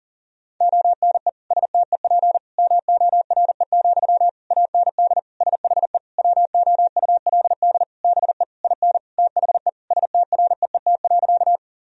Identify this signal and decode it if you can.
Morse code, heard as a keyed tone, about 40 wpm, 700 Hz. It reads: ONE STEP MORE, AND SHE WOULD BE IN THE STREET.
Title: Sentences from Top 1000 Words - Review Part 3 - 40wpm